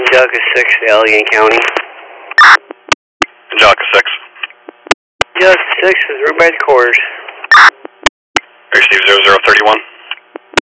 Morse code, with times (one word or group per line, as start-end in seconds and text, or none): none